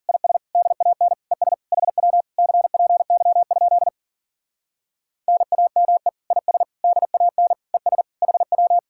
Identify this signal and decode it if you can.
Morse code, heard as a keyed tone, about 49 wpm, 700 Hz. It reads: IS DAN ES HW CPY? NAME IS DAN ES HW
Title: Speed-Racing Common QSO Phrases - 40wpm